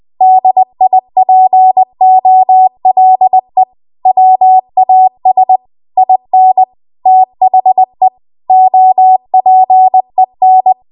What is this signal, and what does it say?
Morse code, heard as a keyed tone, about 20 wpm, 750 Hz. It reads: DIPOLE WAS IN THE OPEN
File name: Archive / 20 WPM / 20WPM